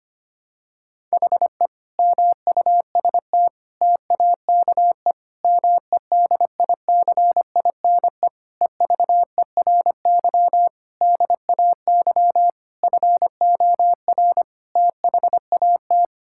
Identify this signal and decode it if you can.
Morse code, heard as a keyed tone, about 25 wpm, 700 Hz. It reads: HE MUST TAKE MEDICINE EVERY DAY FOR THAT
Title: Sentences from Top 1000 Words - Review Part 6 - 25wpm